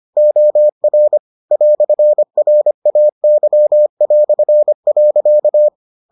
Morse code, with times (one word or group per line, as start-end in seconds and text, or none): 0.2-1.2 OR
1.5-5.7 "RAY".